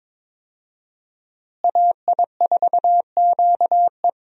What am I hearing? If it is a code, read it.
Morse code, heard as a keyed tone, about 22 wpm, 690 Hz. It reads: AI4QE
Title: USA Call Signs - 5 Characters - Repeated 3x - 22wpm